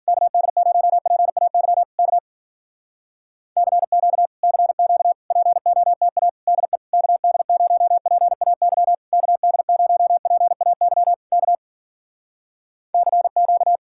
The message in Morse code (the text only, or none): KD0PAY K CQ CQ POTA DE KD0PAY KD0PAY K CQ